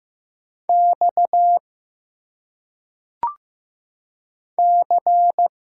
Morse code, heard as a keyed tone, about 15 wpm, 700 Hz. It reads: X C
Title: Single Letters - Phonetic Pronunciation 15wpm